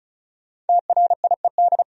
Morse code, heard as a keyed tone, about 48 wpm, spaced as for 42 wpm, 700 Hz. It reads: TRIED
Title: Speed-Racing - Sets of 2 Words 35wpm